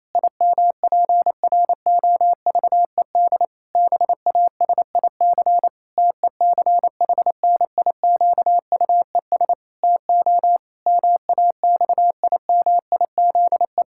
Morse code, heard as a keyed tone, about 28 wpm, 700 Hz. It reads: IMPROVED BASIC TECHNIQUES TO MAXIMIZE
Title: Pangram Challenge v2 - 28wpm